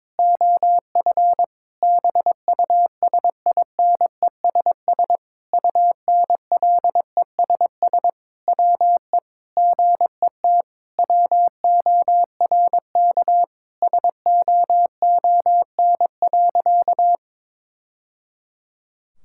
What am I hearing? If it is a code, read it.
Morse code, heard as a keyed tone, about 21 wpm, 700 Hz. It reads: OF BUSINESS UNLESS WE GET WORK SOON.
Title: Sentences from Top 1000 Words - Review Part 6 - 22wpm